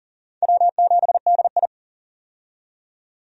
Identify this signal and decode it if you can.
Morse code, heard as a keyed tone, about 40 wpm, 700 Hz. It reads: W7DI
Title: USA Call Signs - 4 Characters - 40wpm